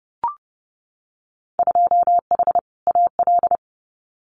Morse code, heard as a keyed tone, about 30 wpm, 700 Hz. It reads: 2H AL
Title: Winter Field Day Exchanges 30wpm